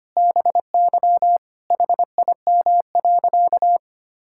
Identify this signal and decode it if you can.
Morse code, heard as a keyed tone, about 25 wpm, 700 Hz. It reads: BY HIM.
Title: Sentences from Top 1000 Words - Review Part 5 - 25wpm